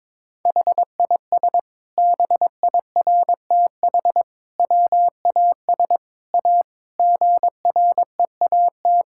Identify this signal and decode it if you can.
Morse code, heard as a keyed tone, about 21 wpm, 700 Hz. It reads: HIS BIRTH WAS A GREAT
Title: Sentences from Top 1000 Words - Review Part 5 - 22wpm